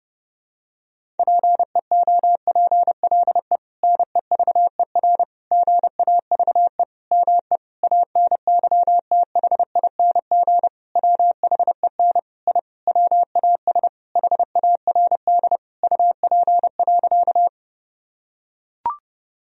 Morse code, heard as a keyed tone, about 30 wpm, 700 Hz. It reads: PEOPLE NEVER GAVE ME ANYTHING WHEN I WAS HARD UP.
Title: Sentences from Top 1000 Words - Review Part 1 - 30wpm